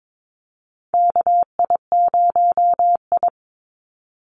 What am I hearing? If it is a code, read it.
Morse code, heard as a keyed tone, about 22 wpm, 700 Hz. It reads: KI0I
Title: Call Signs Repeated 2x - 22wpm